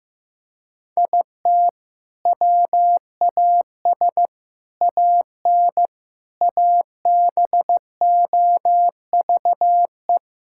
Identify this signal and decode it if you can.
Morse code, heard as a keyed tone, about 15 wpm, 700 Hz. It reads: IT WAS AN ABOVE
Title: Sentences from Top 1000 Words - Review Part 6 - 15wpm